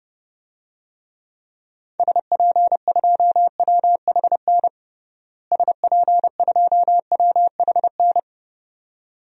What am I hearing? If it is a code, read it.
Morse code, heard as a keyed tone, about 29 wpm, 700 Hz. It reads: SP2WHN SP2WHN
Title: DXCC-100 Call Signs - 6 Characters - Repeated 2x - 30wpm